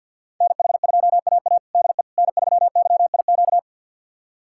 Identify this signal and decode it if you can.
Morse code, heard as a keyed tone, about 53 wpm, 700 Hz. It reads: NH2AA DE N3YIQ
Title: USA Callsign DE Callsign - 5 Characters - 50wpm